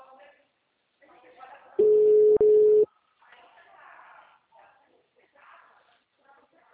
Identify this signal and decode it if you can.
Ringback tone (Europe, 425 Hz), with speech.